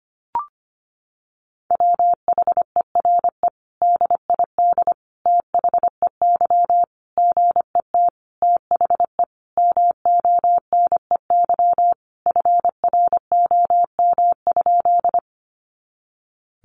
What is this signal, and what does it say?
Morse code, heard as a keyed tone, about 25 wpm, 700 Hz. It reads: WHERE DID THEY GET THE MONEY FROM?
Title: Sentences from Top 400 Words 25wpm